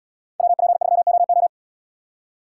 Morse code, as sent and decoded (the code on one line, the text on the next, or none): .-- .--. ...-- --- ---
WP3OO